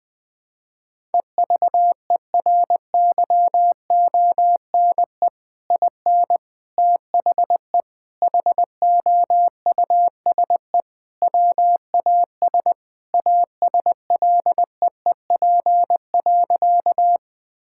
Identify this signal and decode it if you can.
Morse code, heard as a keyed tone, about 20 wpm, 700 Hz. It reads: EVERYONE IN THE HOUSE WAS ASLEEP.
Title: Sentences with 3-Word Phrases - Spoken Beginning - 20wpm